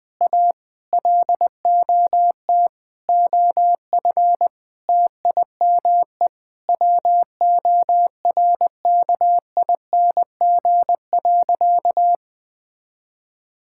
Morse code, heard as a keyed tone, about 21 wpm, 700 Hz. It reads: A LOT OF TIME WORKING.
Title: Sentences with 5-Word Phrases - 20wpm